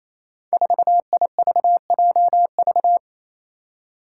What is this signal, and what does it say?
Morse code, heard as a keyed tone, about 27 wpm, 700 Hz. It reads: 4IVJV
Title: Five Challenging Letter-Number Combinations 28wpm